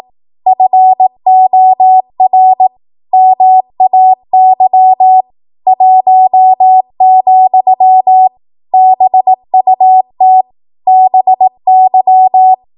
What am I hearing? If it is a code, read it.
Morse code, heard as a keyed tone, about 18 wpm, 750 Hz. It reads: FOR MAY 1, BUT BY
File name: Morse / 18WPM